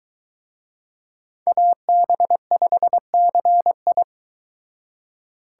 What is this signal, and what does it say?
Morse code, heard as a keyed tone, about 23 wpm, 700 Hz. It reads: AB5CI